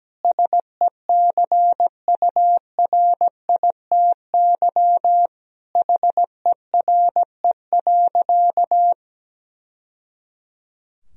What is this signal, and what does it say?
Morse code, heard as a keyed tone, about 17 wpm, 700 Hz. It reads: SECURITY HERE.
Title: Sentences from Top 1000 Words - Review Part 4 - 17wpm